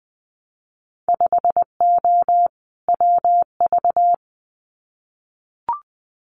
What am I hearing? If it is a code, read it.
Morse code, heard as a keyed tone, about 20 wpm, 700 Hz. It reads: 5O WV
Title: Winter Field Day Exchanges Encore 20wpm